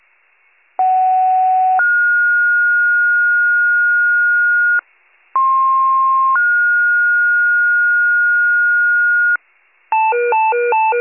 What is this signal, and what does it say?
Two-tone page, heard as a one-second tone, then a three-second tone, about 750 then 1430 Hz, about 1060 then 1430 Hz. It is heard from 0.8 to 9.4 s.
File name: pager / 57 FIRE / wcalert